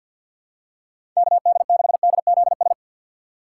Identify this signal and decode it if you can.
Morse code, heard as a keyed tone, about 52 wpm, 700 Hz. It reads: KD6DZS
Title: USA Call Signs - 6 Characters - 3x Character Spacing - 50wpm